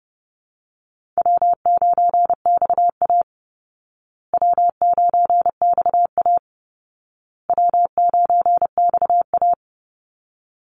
Morse code, heard as a keyed tone, about 30 wpm, 700 Hz. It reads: W9XA W9XA W9XA
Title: Easy Call Signs Repeated 3x - 30wpm